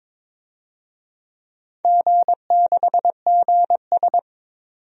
Morse code, heard as a keyed tone, about 22 wpm, 700 Hz. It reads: G6GS
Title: DXCC-100 Call Signs - 4 Characters - 22wpm